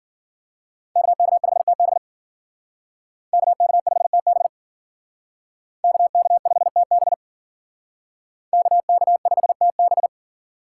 Morse code, heard as a keyed tone, about 50 wpm, 700 Hz. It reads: KK5TB KK5TB KK5TB KK5TB
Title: Speed Racing - USA Call Signs 40wpm